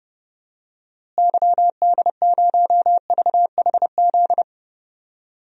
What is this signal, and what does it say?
Morse code, heard as a keyed tone, about 30 wpm, 700 Hz. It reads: YD0VHZ
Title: DXCC-100 Call Signs - 6 Characters - Repeated 2x - 30wpm